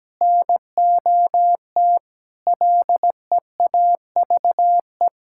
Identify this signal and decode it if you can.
Morse code, heard as a keyed tone, about 17 wpm, 700 Hz. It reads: NOT LEAVE
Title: Sentences from Top 1000 Words - Review Part 6 - 17wpm